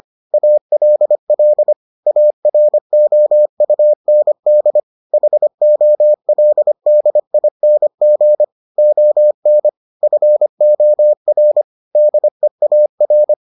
Morse code, heard as a keyed tone, about 25 wpm, 600 Hz. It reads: ALL AROUND HOLDING ON FOR DEAR